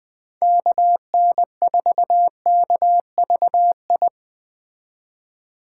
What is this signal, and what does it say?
Morse code, heard as a keyed tone, about 21 wpm, 700 Hz. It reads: KN4KVI